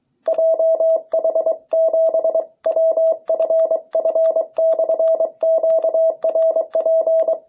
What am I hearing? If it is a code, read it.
Morse code, heard as a keyed tone, about 23 wpm, 660 Hz. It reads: JH7WFF/QRP